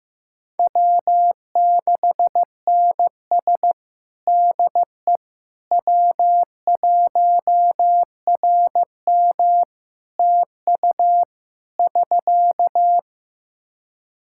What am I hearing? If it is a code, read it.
Morse code, heard as a keyed tone, about 15 wpm, 700 Hz. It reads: W6NS DE W1RM TU <SK>
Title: K1USN SST QSO Practice 15wpm